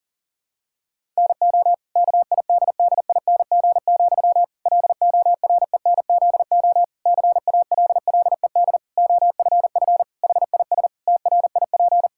Morse code, heard as a keyed tone, about 40 wpm, 700 Hz. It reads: NO KIDDING, LORENZO CALLED OFF HIS TRIP
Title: Pangram Challenge v2 - 40wpm